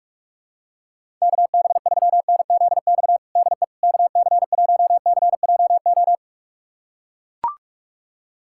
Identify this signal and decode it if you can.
Morse code, heard as a keyed tone, about 41 wpm, 700 Hz. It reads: KB3NZX DE KC1CJO
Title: USA Callsign DE Callsign 45wpm